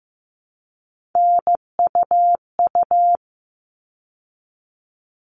Morse code, heard as a keyed tone, about 15 wpm, 700 Hz. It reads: NUU
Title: All Cut Numbers - 3-Digit 15wpm